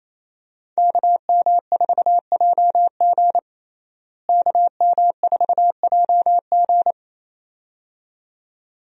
Morse code, heard as a keyed tone, about 27 wpm, 700 Hz. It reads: KM4JG KM4JG
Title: USA Call Signs - 5 Characters - Repeated 2x - 28wpm